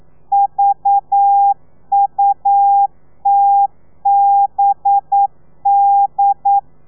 Morse code, heard as a keyed tone, about 9 wpm, 800 Hz. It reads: VUTBD
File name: telegrafia / i2tfj / Lez4 40